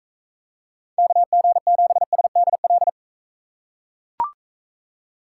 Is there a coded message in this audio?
Yes: Morse code from 1.0 to 2.9 s, heard as a keyed tone, about 42 wpm, 700 Hz. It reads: KG7SDL